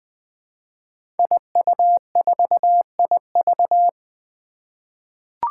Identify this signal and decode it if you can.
Morse code, heard as a keyed tone, about 20 wpm, 700 Hz. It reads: IU4IV
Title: Five Challenging Letter-Number Combinations 20wpm